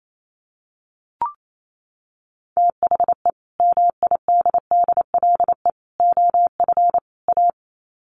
Morse code, heard as a keyed tone, about 28 wpm, 700 Hz. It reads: THE MIDDLE OF A
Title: Sets of 4 Words 28wpm